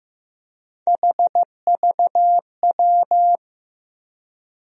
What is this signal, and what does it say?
Morse code, heard as a keyed tone, about 15 wpm, 700 Hz. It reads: HVW
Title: USA Call Sign Postfixes - 3 Letters 15wpm